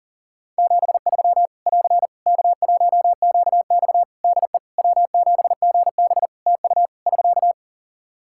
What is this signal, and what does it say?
Morse code, heard as a keyed tone, about 37 wpm, 700 Hz. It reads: 73 <AR> K1QX DE W7GB TU <SK>
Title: K1USN SST QSO Practice 40wpm